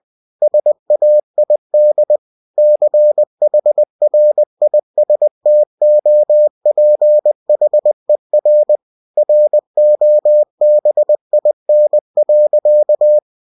Morse code, heard as a keyed tone, about 20 wpm, 600 Hz. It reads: SAID CHRISTOPHER ROBIN.
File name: cw-books / winnie-the-pooh / winnie-20wpm-page-0043